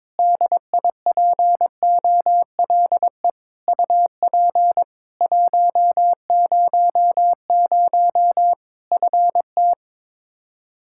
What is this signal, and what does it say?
Morse code, heard as a keyed tone, about 22 wpm, 700 Hz. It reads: DIPOLE UP 100 FT
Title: K1USN SST QSO Practice 22wpm